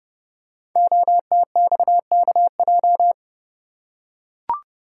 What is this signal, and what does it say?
Morse code, heard as a keyed tone, about 28 wpm, 700 Hz. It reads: OTXKJ
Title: Code Groups - 5 Letter:Numbers - 30wpm